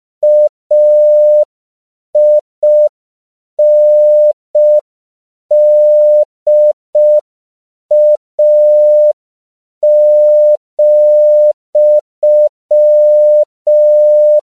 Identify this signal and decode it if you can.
Morse code, heard as a keyed tone, about 5 wpm, 600 Hz. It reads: AINDA,
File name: CW / 5PPM